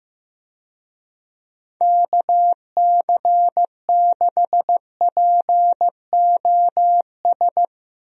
Morse code, heard as a keyed tone, about 15 wpm, 700 Hz. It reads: KC6POS
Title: USA Call Signs - 6 Characters - Repeated 2x - 15wpm